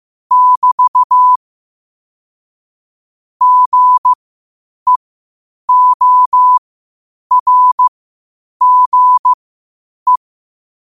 Morse code, heard as a keyed tone, about 15 wpm, spaced as for 8 wpm, 1000 Hz. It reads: <BT> GEORGE
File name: qotdfiles / 8wpm